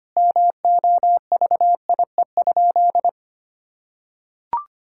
Morse code, heard as a keyed tone, about 25 wpm, 700 Hz. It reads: MOVIE?
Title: Sentences from Top 1000 Words - Review Part 5 - 25wpm